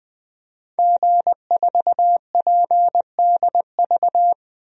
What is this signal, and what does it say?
Morse code, heard as a keyed tone, about 21 wpm, 700 Hz. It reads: G4PDV
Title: DXCC-100 Call Signs - 5 Characters - 20wpm